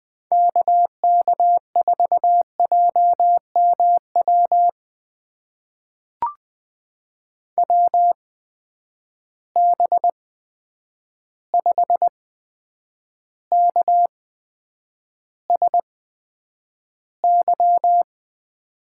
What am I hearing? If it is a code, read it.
Morse code, heard as a keyed tone, about 20 wpm, spaced as for 9 wpm, 700 Hz. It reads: KK4JMW WB5KSY
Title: USA Call Signs - 6 Characters - 8x Character Spacing - 20wpm